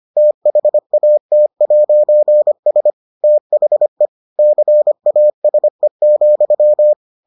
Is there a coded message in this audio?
Yes: Morse code from 0.2 to 6.9 s, heard as a keyed tone, about 25 wpm, 600 Hz. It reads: THAT'S THE CASE,